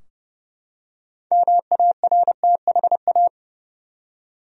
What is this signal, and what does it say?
Morse code, heard as a keyed tone, about 32 wpm, 700 Hz. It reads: MARTHA
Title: USA Common Names - 30wpm